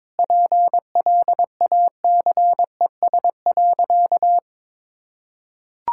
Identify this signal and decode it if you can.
Morse code, heard as a keyed tone, about 22 wpm, 700 Hz. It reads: PLACES.
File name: practice / 22wpm / Sentences with 3-Word Phrases - 22wpm